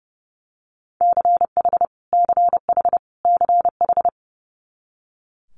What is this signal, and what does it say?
Morse code, heard as a keyed tone, about 30 wpm, 700 Hz. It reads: CH CH CH
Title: Multi-letter Phonemes Repeated 3x 30wpm